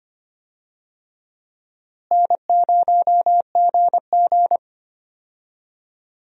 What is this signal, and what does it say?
Morse code, heard as a keyed tone, about 25 wpm, 700 Hz. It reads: N0GG
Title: USA Call Signs - 4 Characters - Repeated 3x - 25wpm